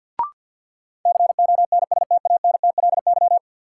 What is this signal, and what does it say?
Morse code, heard as a keyed tone, about 49 wpm, 700 Hz. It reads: CONSTANTLY
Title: Sentences from Top 1600 Words - New Word Drill Alt - 50wpm